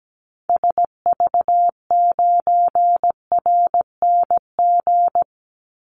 Morse code, heard as a keyed tone, about 17 wpm, 700 Hz. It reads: SV9RNG
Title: Call Signs Repeated 3x - 17wpm